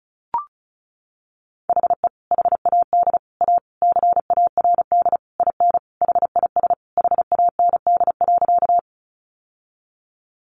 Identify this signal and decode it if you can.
Morse code, heard as a keyed tone, about 34 wpm, 700 Hz. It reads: HE HAD A CARD IN HIS HAND.
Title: Sentences from Top 600 Words 35wpm